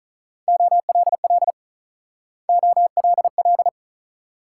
Morse code, heard as a keyed tone, about 46 wpm, 700 Hz. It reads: OLL OLL